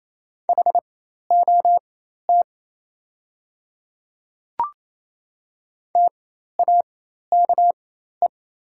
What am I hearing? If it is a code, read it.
Morse code, heard as a keyed tone, about 28 wpm, spaced as for 13 wpm, 700 Hz. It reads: HOT TAKE
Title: 4x Character Spacing - Top 500 Words - 28wpm